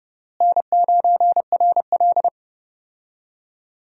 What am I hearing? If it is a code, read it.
Morse code, heard as a keyed tone, about 32 wpm, 700 Hz. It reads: N9RL
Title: USA Call Signs - 4 Characters - Repeated 2x - 30wpm